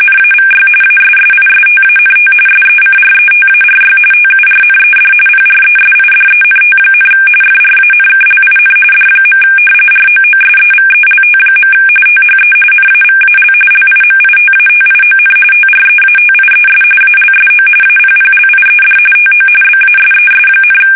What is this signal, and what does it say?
RTTY, about 45.45 baud. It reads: NAVY ASSISTANT SECRETARY TURNED PRESIDENT, FRANKLIN D. ROOSEVELT. THE NAVAL LIMITATION TREATIES ALSO APPLIED TO BASES,
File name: fsk850